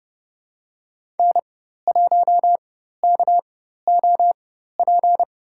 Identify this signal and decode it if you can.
Morse code, heard as a keyed tone, about 30 wpm, spaced as for 14 wpm, 700 Hz. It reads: N1KOP
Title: USA Call Signs - 5 Characters - 4x Character Spacing - 30wpm